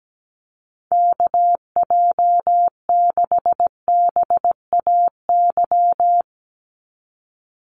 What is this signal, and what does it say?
Morse code, heard as a keyed tone, about 17 wpm, 700 Hz. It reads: KJ6BAY
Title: USA Call Signs 17wpm